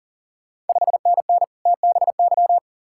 Morse code, heard as a keyed tone, about 42 wpm, 690 Hz. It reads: 5NN TBY
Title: Ontario QSO Party Exchange 40wpm